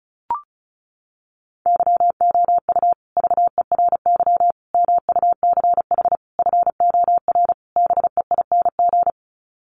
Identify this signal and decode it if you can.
Morse code, heard as a keyed tone, about 34 wpm, 700 Hz. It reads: YOU VERY MUCH FOR BEING
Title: Sets of 5 Words Encore 35wpm